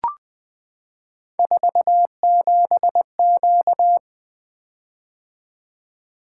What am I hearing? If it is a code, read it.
Morse code, heard as a keyed tone, about 20 wpm, 700 Hz. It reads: 47Q
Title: Three Letter-Number Combinations - 20wpm